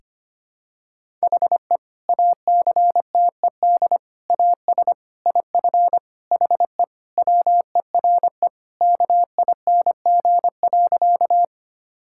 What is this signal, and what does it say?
Morse code, heard as a keyed tone, about 25 wpm, 700 Hz. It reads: HE ACTED AS IF HE WERE KING.
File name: practice / 25wpm / Sentences with 4-Word Phrases - 25wpm